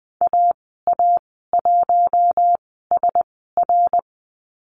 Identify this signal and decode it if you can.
Morse code, heard as a keyed tone, about 20 wpm, spaced as for 15 wpm, 700 Hz. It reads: AA1SR
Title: 122 Lesson 20g - Number 3 - Call Signs 20wpm